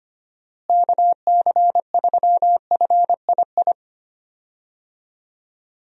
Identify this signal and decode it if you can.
Morse code, heard as a keyed tone, about 25 wpm, 700 Hz. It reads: KC3FII
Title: USA Call Signs - 6 Characters - Repeated 2x - 25wpm